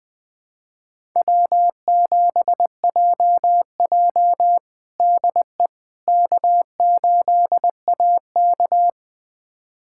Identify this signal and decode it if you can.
Morse code, heard as a keyed tone, about 20 wpm, 700 Hz. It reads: W7JJ DE K8AK
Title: USA Callsign DE Callsign - 4 Characters - 20wpm